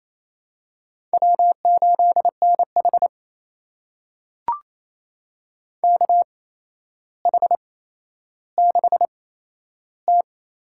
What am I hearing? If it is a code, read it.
Morse code, heard as a keyed tone, about 28 wpm, spaced as for 12 wpm, 700 Hz. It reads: W8NH KH6T
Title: USA Call Signs - 4 Characters - 8x Character Spacing - 28wpm